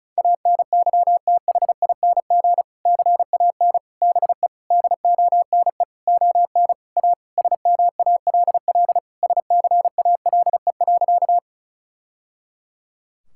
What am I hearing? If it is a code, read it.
Morse code, heard as a keyed tone, about 36 wpm, 700 Hz. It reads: ANYTHING CAN BE DONE ON A SMALL SCALE.
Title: Sentences from Top 1000 Words - Review Part 5 - 35wpm